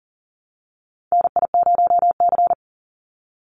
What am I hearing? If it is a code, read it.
Morse code, heard as a keyed tone, about 40 wpm, 700 Hz. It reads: NI0C
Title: Call Signs Repeated 3x - 40wpm